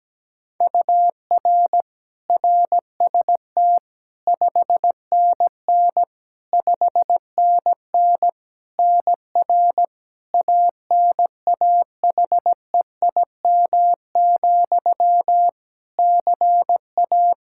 Morse code, heard as a keyed tone, about 17 wpm, 700 Hz. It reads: UR RST 5NN 5NN NR ANAHEIM, CA
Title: Practice QSO - Exchange 1 - Beginning - 17wpm